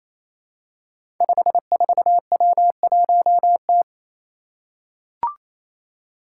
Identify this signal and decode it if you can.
Morse code, heard as a keyed tone, about 28 wpm, 700 Hz. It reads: 54W1T
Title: Five Challenging Letter-Number Combinations 28wpm